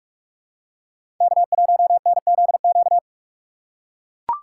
Morse code, heard as a keyed tone, about 43 wpm, 700 Hz. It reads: K1NZQ
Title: USA Call Signs - 3x Character Spacing - 45wpm